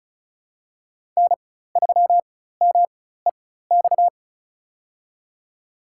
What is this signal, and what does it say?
Morse code, heard as a keyed tone, about 36 wpm, spaced as for 16 wpm, 700 Hz. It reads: N3MEX
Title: USA Call Signs - 5 Characters - 4x Character Spacing - 35wpm